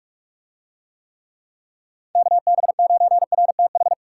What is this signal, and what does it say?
Morse code, heard as a keyed tone, about 49 wpm, 700 Hz. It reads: KB9RTH